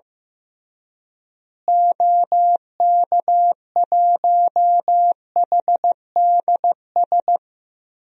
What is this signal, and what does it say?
Morse code, heard as a keyed tone, about 16 wpm, 700 Hz. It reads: OK1HDS
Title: DXCC-100 Call Signs - 6 Characters - Repeated 3x - 15wpm